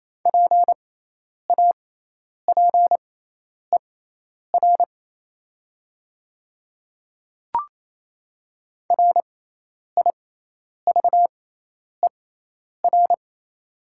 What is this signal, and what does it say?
Morse code, heard as a keyed tone, about 28 wpm, spaced as for 10 wpm, 700 Hz. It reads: PAPER RIVER
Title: 6x Character Spacing - Top 300 Words - 28wpm